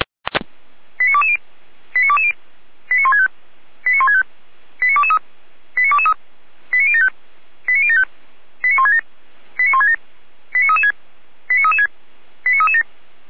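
Five-tone selective call (ZVEI), heard as quick runs of five tones, tones of about 70 ms. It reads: 89220 89220 89165 89165 89223 89223 89075 89075 89167 89167 89336 89336 89337
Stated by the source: ZVEI five-tone selective call 89017 received on SDR.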